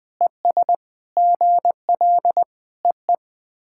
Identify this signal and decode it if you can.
Morse code, heard as a keyed tone, about 20 wpm, 700 Hz. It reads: ES GL EE
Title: SOTA QSO Practice 20wpm